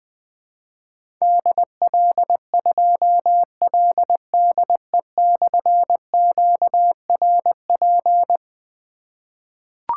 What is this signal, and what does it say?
Morse code, heard as a keyed tone, about 20 wpm, 700 Hz. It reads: DL2LDE/QRP
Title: Difficult Call Signs 20wpm